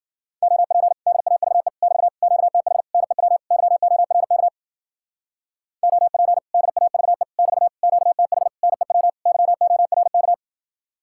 Morse code, heard as a keyed tone, about 48 wpm, 700 Hz. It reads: OP DAVE <BT> QTH NEW YORK OP DAVE <BT> QTH NEW YORK
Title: Speed-Racing Common QSO Phrases Encore - 40wpm